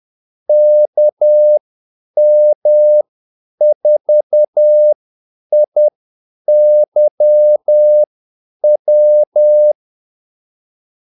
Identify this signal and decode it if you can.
Morse code, heard as a keyed tone, about 10 wpm, spaced as for 8 wpm, 600 Hz. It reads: KM4IYW
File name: Callsigns-10-8wpm